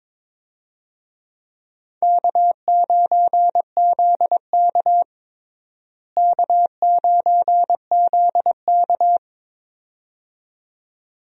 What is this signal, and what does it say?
Morse code, heard as a keyed tone, about 22 wpm, 700 Hz. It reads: K9ZK K9ZK
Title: USA Call Signs - 4 Characters - Repeated 2x - 22wpm